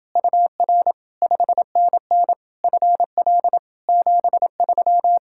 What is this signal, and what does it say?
Morse code, heard as a keyed tone, about 29 wpm, 700 Hz. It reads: UR 5NN FL 73